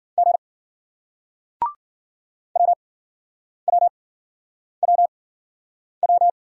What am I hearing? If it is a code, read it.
Morse code, heard as a keyed tone, about 53 wpm, spaced as for 9 wpm, 700 Hz. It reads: G WWWW